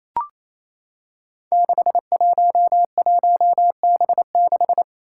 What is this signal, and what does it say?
Morse code, heard as a keyed tone, about 28 wpm, 700 Hz. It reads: 611B6
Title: Five Challenging Letter-Number Combinations 28wpm